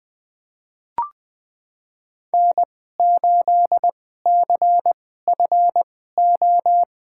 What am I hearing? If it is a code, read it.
Morse code, heard as a keyed tone, about 21 wpm, spaced as for 15 wpm, 700 Hz. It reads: N8CFO
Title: USA Call Signs - 5 Characters - 2x Character Spacing - 20wpm